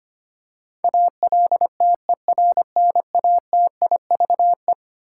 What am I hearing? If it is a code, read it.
Morse code, heard as a keyed tone, about 25 wpm, 700 Hz. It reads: ALTERNATIVE
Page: morse-code-ninja-practice-25wpm